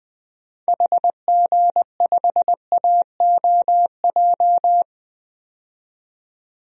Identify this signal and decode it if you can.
Morse code, heard as a keyed tone, about 21 wpm, 700 Hz. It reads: HG5AOJ
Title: DXCC-100 Call Signs - 6 Characters - 20wpm